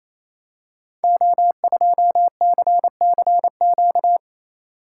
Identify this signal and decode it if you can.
Morse code, heard as a keyed tone, about 29 wpm, 700 Hz. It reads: O2CCQ
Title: Code Groups - 5 Letter:Numbers - 28wpm